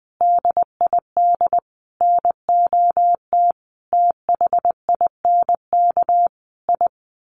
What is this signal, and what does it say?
Morse code, heard as a keyed tone, about 19 wpm, 700 Hz. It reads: DID NOT THINK I
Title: Sets of 5 Words Encore 20wpm